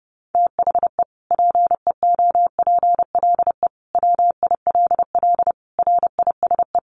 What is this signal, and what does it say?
Morse code, heard as a keyed tone, about 28 wpm, 700 Hz. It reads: THE PEOPLE WILL RISE
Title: Sentences from Top 900 Words 30wpm